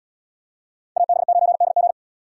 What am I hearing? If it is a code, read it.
Morse code, heard as a keyed tone, about 77 wpm, 700 Hz. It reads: AB9DG